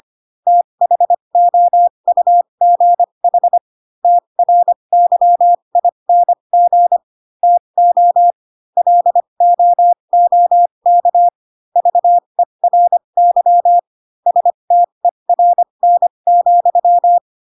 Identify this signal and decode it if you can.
Morse code, heard as a keyed tone, about 25 wpm, spaced as for 22 wpm, 700 Hz. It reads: THOUGH TRYING TO LOOK VERY STERN,